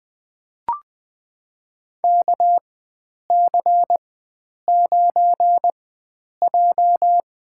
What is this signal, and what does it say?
Morse code, heard as a keyed tone, about 20 wpm, spaced as for 9 wpm, 700 Hz. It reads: KC9J
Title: USA Call Signs - 4x Character Spacing - 20wpm